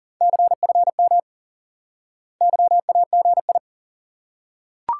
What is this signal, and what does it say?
Morse code, heard as a keyed tone, about 38 wpm, 700 Hz. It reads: CFM YAGI
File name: IQE Course 3 - Lesson 89 - YAGI - Two Elements - 40wpm